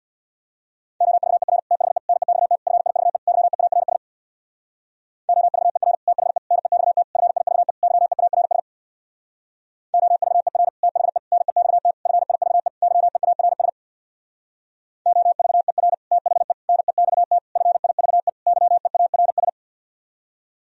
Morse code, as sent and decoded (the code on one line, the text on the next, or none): --- ...- . .-. / - .... / -. -..- - / ..-. .. ...- . / -.-- . .- .-. ... / --- ...- . .-. / - .... / -. -..- - / ..-. .. ...- . / -.-- . .- .-. ... / --- ...- . .-. / - .... / -. -..- - / ..-. .. ...- . / -.-- . .- .-. ... / --- ...- . .-. / - .... / -. -..- - / ..-. .. ...- . / -.-- . .- .-. ...
OVER THE NEXT FIVE YEARS OVER THE NEXT FIVE YEARS OVER THE NEXT FIVE YEARS OVER THE NEXT FIVE YEARS